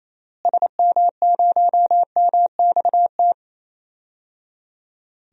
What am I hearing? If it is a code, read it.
Morse code, heard as a keyed tone, about 28 wpm, 700 Hz. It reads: SM0MXT
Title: DXCC-100 Call Signs - 6 Characters - Repeated 3x - 28wpm